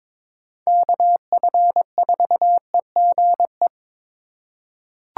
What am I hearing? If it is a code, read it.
Morse code, heard as a keyed tone, about 22 wpm, 700 Hz. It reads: KF4EGE
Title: USA Call Signs - 8x Character Spacing - 22wpm